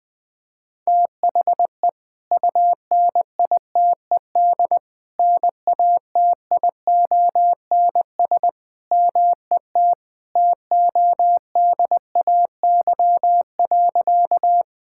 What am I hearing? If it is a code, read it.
Morse code, heard as a keyed tone, about 21 wpm, 700 Hz. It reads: THE UNITED NATIONS MET TODAY.
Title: Sentences with 3-Word Phrases - Spoken Beginning - 20wpm